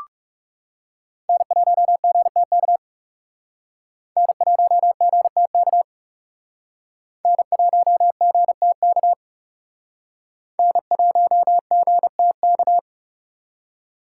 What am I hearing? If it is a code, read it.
Morse code, heard as a keyed tone, about 38 wpm, 700 Hz. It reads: N1GTK N1GTK N1GTK N1GTK